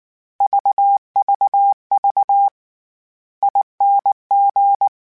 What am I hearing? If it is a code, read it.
Morse code, heard as a keyed tone, about 19 wpm, 800 Hz. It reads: VVV ING
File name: Special INGs / Special INGs-mp3s / VerySlowToFaster Ing @18